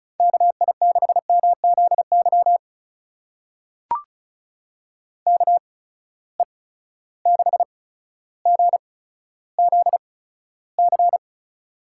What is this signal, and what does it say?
Morse code, heard as a keyed tone, about 36 wpm, spaced as for 15 wpm, 700 Hz. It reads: KI6MZY KE6GZC
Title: USA Call Signs - 6 Characters - 8x Character Spacing - 35wpm